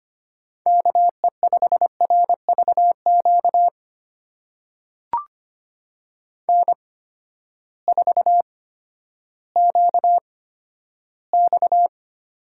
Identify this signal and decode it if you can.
Morse code, heard as a keyed tone, about 25 wpm, spaced as for 13 wpm, 700 Hz. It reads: KE5RVQ N4QX
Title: USA Call Signs - 8x Character Spacing - 25wpm